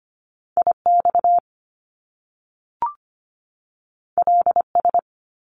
Morse code, heard as a keyed tone, about 25 wpm, 700 Hz. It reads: IX LS
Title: Top Two Letter Combinations in Words 25wpm